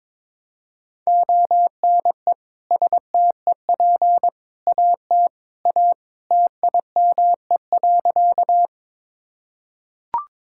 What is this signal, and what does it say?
Morse code, heard as a keyed tone, about 22 wpm, 700 Hz. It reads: ONE STEP AT A TIME.
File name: practice / 22wpm / Sentences with 3-Word Phrases - 22wpm